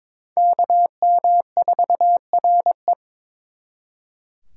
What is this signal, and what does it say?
Morse code, heard as a keyed tone, about 26 wpm, 700 Hz. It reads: KM4RE
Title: Speed Racing - USA 2x2 Call Signs 22wpm